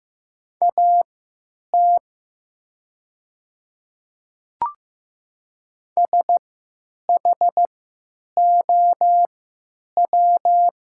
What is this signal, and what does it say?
Morse code, heard as a keyed tone, about 15 wpm, spaced as for 9 wpm, 700 Hz. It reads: AT SHOW
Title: 3x Character Spacing - Top 500 Words - 15wpm